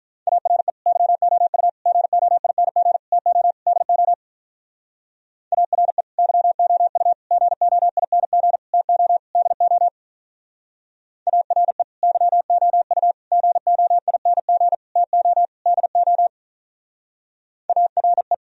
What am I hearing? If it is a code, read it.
Morse code, heard as a keyed tone, about 45 wpm, 700 Hz. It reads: ARE YOU GOING TO DO ARE YOU GOING TO DO ARE YOU GOING TO DO ARE